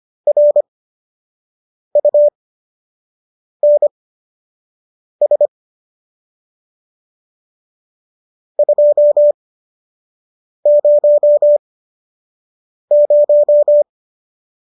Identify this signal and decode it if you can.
Morse code, heard as a keyed tone, about 25 wpm, spaced as for 6 wpm, 590 Hz. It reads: RUNS 200